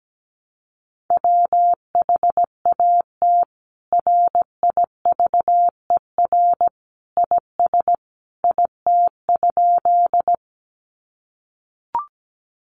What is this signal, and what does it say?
Morse code, heard as a keyed tone, about 17 wpm, 700 Hz. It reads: WHAT RIVER IS IT?
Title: Sentences from Top 300 Words 17wpm